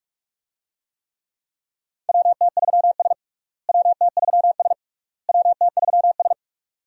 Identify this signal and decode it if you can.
Morse code, heard as a keyed tone, about 46 wpm, 700 Hz. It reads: WT3S WT3S WT3S
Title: USA Call Signs - 4 Characters - Repeated 3x - 45wpm